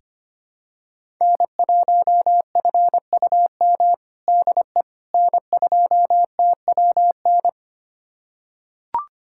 Morse code, heard as a keyed tone, about 25 wpm, 700 Hz. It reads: N1FUM DE N2TWN
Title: USA Callsign DE Callsign - 5 Characters - 25wpm